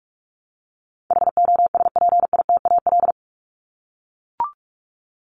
Morse code, heard as a keyed tone, about 44 wpm, 700 Hz. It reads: HOSPITAL
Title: Top 1751-2000 Words 45wpm